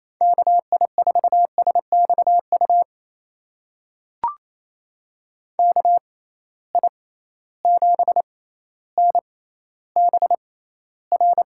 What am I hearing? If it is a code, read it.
Morse code, heard as a keyed tone, about 28 wpm, spaced as for 14 wpm, 700 Hz. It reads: KI4SXU KI7NBR
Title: USA Call Signs - 6 Characters - 6x Character Spacing - 28wpm